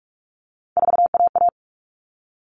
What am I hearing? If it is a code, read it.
Morse code, heard as a keyed tone, about 44 wpm, 700 Hz. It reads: 4AA
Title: All Cut Numbers - 3-Digit 45wpm